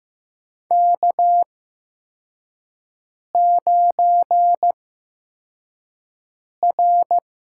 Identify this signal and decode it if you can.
Morse code, heard as a keyed tone, about 15 wpm, spaced as for 4 wpm, 700 Hz. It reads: K9R
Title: USA Call Signs - 8x Character Spacing - 15wpm